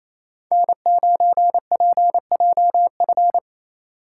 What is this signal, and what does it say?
Morse code, heard as a keyed tone, about 36 wpm, 700 Hz. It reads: N9PJF